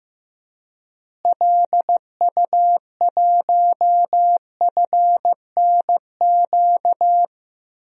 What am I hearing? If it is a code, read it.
Morse code, heard as a keyed tone, about 15 wpm, 700 Hz. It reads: LU1FNQ
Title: DXCC-100 Call Signs - 6 Characters - Repeated 2x - 15wpm